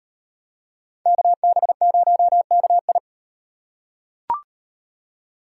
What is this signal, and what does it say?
Morse code, heard as a keyed tone, about 38 wpm, 700 Hz. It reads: KB0KI